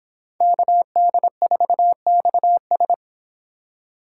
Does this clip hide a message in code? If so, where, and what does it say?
Yes: Morse code from 0.4 to 2.9 s, heard as a keyed tone, about 25 wpm, 700 Hz. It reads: KD4XS